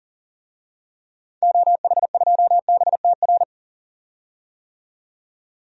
Morse code, heard as a keyed tone, about 40 wpm, 700 Hz. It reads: OH2BTR
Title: DXCC-100 Call Signs - 6 Characters - 40wpm